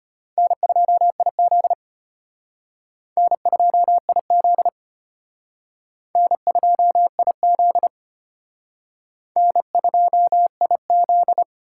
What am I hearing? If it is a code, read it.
Morse code, heard as a keyed tone, about 31 wpm, 700 Hz. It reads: N2IZ N2IZ N2IZ N2IZ
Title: Speed Racing - USA 1x2 Call Signs 25wpm